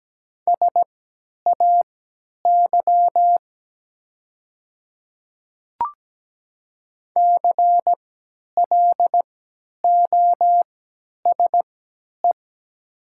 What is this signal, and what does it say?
Morse code, heard as a keyed tone, about 17 wpm, spaced as for 10 wpm, 700 Hz. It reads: SAY CLOSE
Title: 3x Character Spacing - Top 300 Words - 17wpm